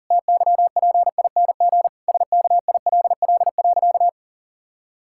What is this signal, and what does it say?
Morse code, heard as a keyed tone, about 37 wpm, 700 Hz. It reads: TYPING SKILL.
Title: Pangram Challenge v2 - 40wpm